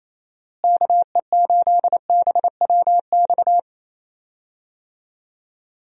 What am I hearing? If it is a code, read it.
Morse code, heard as a keyed tone, about 28 wpm, 700 Hz. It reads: KE8BWX
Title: USA Call Signs - 6 Characters - Repeated 3x - 28wpm